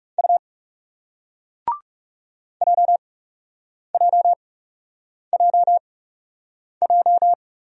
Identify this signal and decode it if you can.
Morse code, heard as a keyed tone, about 40 wpm, spaced as for 8 wpm, 700 Hz. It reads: U JJJJ